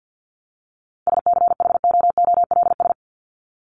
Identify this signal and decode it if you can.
Morse code, heard as a keyed tone, about 50 wpm, 700 Hz. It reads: SCHOOLS
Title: ES:S Suffix - Top 100 Words with Base Words 50wpm